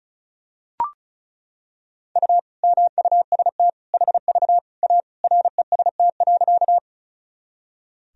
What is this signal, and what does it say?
Morse code, heard as a keyed tone, about 36 wpm, 700 Hz. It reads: U MUST HV A REST.
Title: Sentences from Top 500 Words and Abbreviations 35wpm